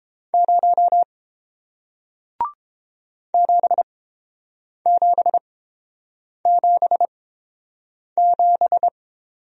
Morse code, heard as a keyed tone, about 30 wpm, spaced as for 7 wpm, 700 Hz. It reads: 0 7777